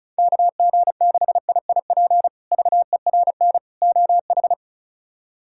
Morse code, heard as a keyed tone, about 34 wpm, 700 Hz. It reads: KG6IIP VERN OH
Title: K1USN SST QSO Practice - 35wpm